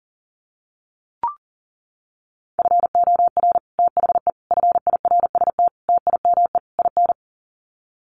Morse code, heard as a keyed tone, about 40 wpm, 700 Hz. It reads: FOR THE FIRST TIME IN
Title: Sets of 5 Words 40wpm